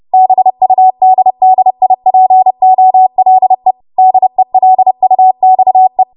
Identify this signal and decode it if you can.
Morse code, heard as a keyed tone, about 30 wpm, 750 Hz. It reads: BUDDIPOLE DELUXE
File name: Morse / 30WPM